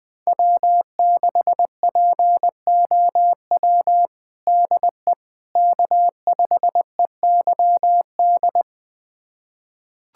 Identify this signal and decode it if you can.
Morse code, heard as a keyed tone, about 20 wpm, 700 Hz. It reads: W6POW DE K5EYD
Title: USA Callsign DE Callsign 20wpm